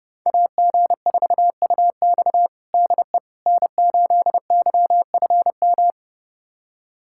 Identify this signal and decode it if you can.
Morse code, heard as a keyed tone, about 30 wpm, 700 Hz. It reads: AG4UX DE N8YFM
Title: USA Callsign DE Callsign - 5 Characters - 30wpm